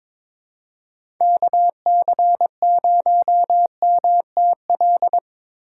Morse code, heard as a keyed tone, about 21 wpm, 700 Hz. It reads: KC0MTL
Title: USA Call Signs - 3x Character Spacing - 22wpm